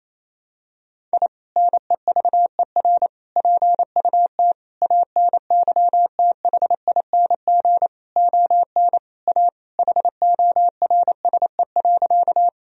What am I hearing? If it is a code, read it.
Morse code, heard as a keyed tone, about 28 wpm, 700 Hz. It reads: I NEVER PUT ANYTHING ON A HORSE.
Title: Sentences from Top 1000 Words - Review Part 1 - 28wpm